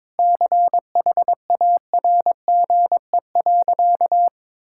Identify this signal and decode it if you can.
Morse code, heard as a keyed tone, about 21 wpm, 700 Hz. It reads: CHARGE.
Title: Sentences from Top 1000 Words - Review Part 6 - 22wpm